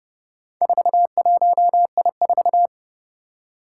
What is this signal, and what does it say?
Morse code, heard as a keyed tone, about 30 wpm, 700 Hz. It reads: 41I4
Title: Four Challenging Letter-Number Combinations 30wpm